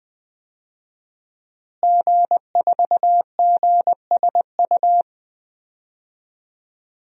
Morse code, heard as a keyed tone, about 20 wpm, 700 Hz. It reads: G4GSU
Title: DXCC-100 Call Signs - 5 Characters - Repeated 3x - 20wpm